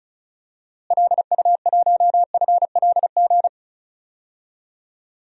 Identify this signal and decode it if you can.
Morse code, heard as a keyed tone, about 36 wpm, 700 Hz. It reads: LU1FLG
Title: DXCC-100 Call Signs - 6 Characters - Repeated 3x - 35wpm